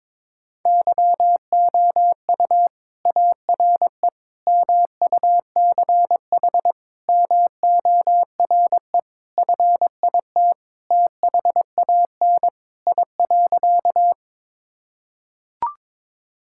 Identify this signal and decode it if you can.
Morse code, heard as a keyed tone, about 22 wpm, 700 Hz. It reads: YOU ARE MUCH MORE FIT THAN I.
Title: Sentences from Top 1000 Words - Review Part 5 - 22wpm